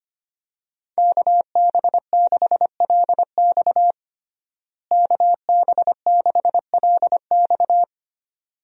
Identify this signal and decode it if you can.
Morse code, heard as a keyed tone, about 25 wpm, 700 Hz. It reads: KB6LX KB6LX
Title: USA Call Signs - 5 Characters - Repeated 2x - 25wpm